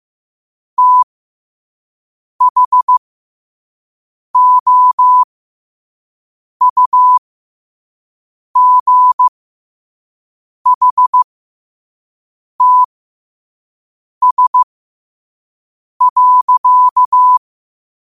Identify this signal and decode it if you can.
Morse code, heard as a keyed tone, about 15 wpm, spaced as for 5 wpm, 1000 Hz. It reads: THOUGHTS.
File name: qotdfiles / 5wpm